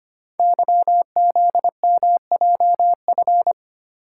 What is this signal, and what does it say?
Morse code, heard as a keyed tone, about 25 wpm, 690 Hz. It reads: YZMJF
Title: Code Groups - 5 Letters - 25wpm